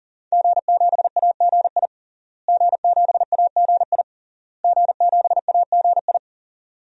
Morse code, heard as a keyed tone, about 37 wpm, 700 Hz. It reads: G7AGI G7AGI G7AGI
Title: DXCC-100 Call Signs - 5 Characters - Repeated 3x - 40wpm